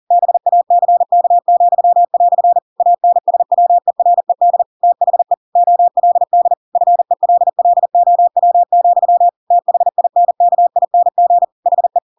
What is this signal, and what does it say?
Morse code, heard as a keyed tone, about 40 wpm, 700 Hz. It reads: BACK," ANSWERED THE OLD FELLOW, THINKING HE
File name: Pinocchio / Pinocchio ch6 40 40 ik1hge